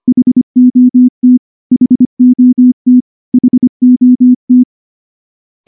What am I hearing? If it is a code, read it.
Morse code, heard as a keyed tone, about 23 wpm, 250 Hz. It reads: HOT HOT HOT
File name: morse / morse-training-25WPM